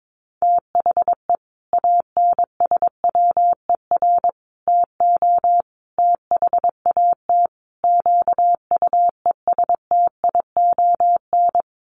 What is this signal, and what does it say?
Morse code, heard as a keyed tone, about 22 wpm, 700 Hz. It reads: THE ANSWER TO THAT QUESTION
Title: Sets of 5 Words Encore2 22wpm